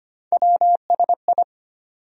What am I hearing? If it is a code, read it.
Morse code, heard as a keyed tone, about 25 wpm, 700 Hz. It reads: WSI
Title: USA Call Sign Postfixes - 3 Letters 25wpm